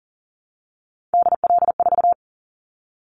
Morse code, heard as a keyed tone, about 40 wpm, 700 Hz. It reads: DL4
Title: Callsign Prefixes for Top 25 DXCC - 40wpm